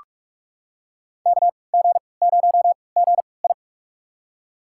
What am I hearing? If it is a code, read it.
Morse code, heard as a keyed tone, about 41 wpm, spaced as for 25 wpm, 700 Hz. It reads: KG0GI
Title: USA Call Signs - 5 Characters - 3x Character Spacing - 45wpm